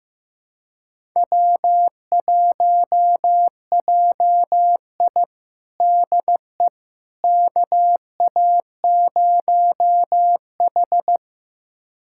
Morse code, heard as a keyed tone, about 15 wpm, 700 Hz. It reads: W1JI DE KA0H
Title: USA Callsign DE Callsign - 4 Characters - 15wpm